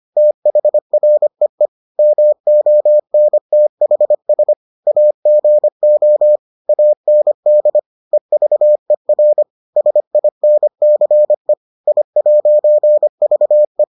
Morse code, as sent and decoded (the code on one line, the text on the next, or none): - .... .-. . / -- --- -. - .... ... / .- --. --- / .- -. -.. / . ...- . .-. / ... .. -. -.-. / .. .----. ...- .
THREE MONTHS AGO AND EVER SINCE I'VE